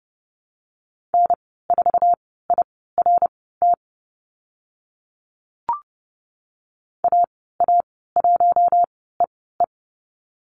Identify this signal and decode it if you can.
Morse code, heard as a keyed tone, about 30 wpm, spaced as for 17 wpm, 700 Hz. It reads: N4IRT AA1EE
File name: Lesson 14g - Number 2 - Call Signs 30wpm